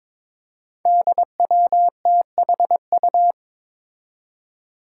Morse code, heard as a keyed tone, about 22 wpm, 700 Hz. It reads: DWTHU
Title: Five Challenging Letter-Number Combinations 22wpm